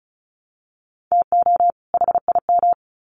Morse code, heard as a keyed tone, about 34 wpm, 700 Hz. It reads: TO HIM
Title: Sets of 2 Words 35wpm